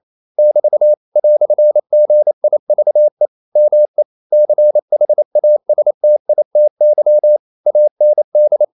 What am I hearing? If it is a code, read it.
Morse code, heard as a keyed tone, about 28 wpm, 600 Hz. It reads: <BT> "GIVE ME CHASTITY AND
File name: CW / Practice / EnglishQuotations0128